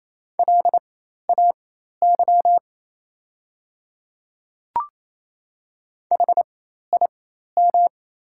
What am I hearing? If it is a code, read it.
Morse code, heard as a keyed tone, about 28 wpm, spaced as for 13 wpm, 700 Hz. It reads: LAY HIM
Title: 4x Character Spacing - Top 500 Words - 28wpm